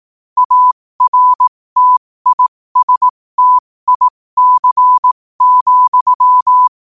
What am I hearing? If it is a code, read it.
Morse code, heard as a keyed tone, about 18 wpm, spaced as for 15 wpm, 1000 Hz. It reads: ARTISTIC,